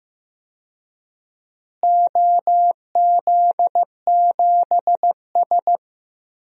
Morse code, heard as a keyed tone, about 15 wpm, 700 Hz. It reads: OZ7S
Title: DXCC-100 Call Signs - 4 Characters - Repeated 2x - 15wpm